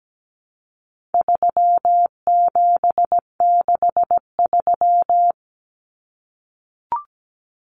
Morse code, heard as a keyed tone, about 17 wpm, 700 Hz. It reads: 3763
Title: Common Cut Numbers - 4-Digit 17wpm